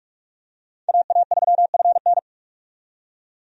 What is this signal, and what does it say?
Morse code, heard as a keyed tone, about 43 wpm, 700 Hz. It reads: AA3FN
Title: USA Call Signs - 5 Characters - 45wpm